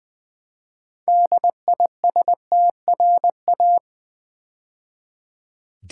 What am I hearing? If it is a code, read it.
Morse code, heard as a keyed tone, about 20 wpm, 700 Hz. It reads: DISTRA
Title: Build A Word - 2 Letters At A Time - 8-Letter Words - 20wpm